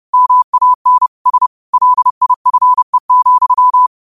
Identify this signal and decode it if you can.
Morse code, heard as a keyed tone, about 30 wpm, 1000 Hz. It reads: MAN S LIFE,
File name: qotdfiles / 30wpm